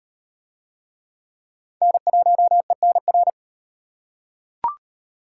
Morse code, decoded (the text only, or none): N1ENR